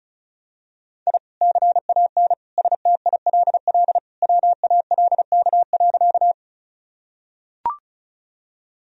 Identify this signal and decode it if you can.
Morse code, heard as a keyed tone, about 34 wpm, 700 Hz. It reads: I CAN STILL WALK.
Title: Sentences from Top 1000 Words - Review Part 1 - 35wpm